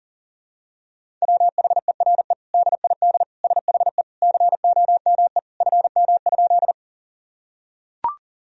Morse code, heard as a keyed tone, about 40 wpm, 700 Hz. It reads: WHERE DID SHE COME FM?
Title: Sentences from Top 200 Words and Abbreviations 40wpm